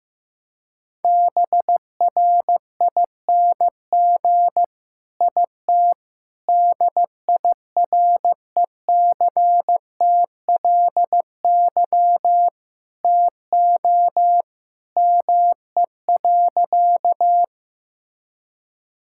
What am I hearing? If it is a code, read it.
Morse code, heard as a keyed tone, about 16 wpm, 700 Hz. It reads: BRING IT DIRECTLY TO ME.
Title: Sentences from Top 1000 Words - Review Part 5 - 15wpm